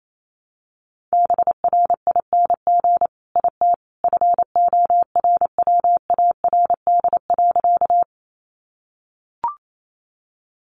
Morse code, heard as a keyed tone, about 28 wpm, 700 Hz. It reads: BRING IT FORWARD.
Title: Sentences from Top 1300 Words 28wpm